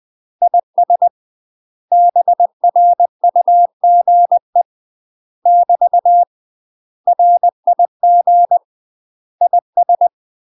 Morse code, heard as a keyed tone, about 20 wpm, 700 Hz. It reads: IS BRUGE <BT> RIG IS
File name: UK MASTER QSOs TRAINING / 20WPM / UK MASTER QSOs 0003